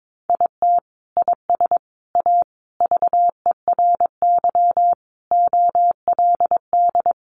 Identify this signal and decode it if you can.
Morse code, heard as a keyed tone, about 22 wpm, 700 Hz. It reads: IT IS A VERY OLD
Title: Sentences from Top 1200 Words 22wpm